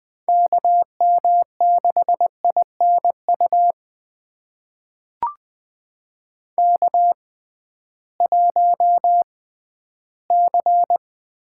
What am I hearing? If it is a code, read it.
Morse code, heard as a keyed tone, about 20 wpm, spaced as for 13 wpm, 700 Hz. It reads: KM6INU K1C
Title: USA Call Signs - 6x Character Spacing - 20wpm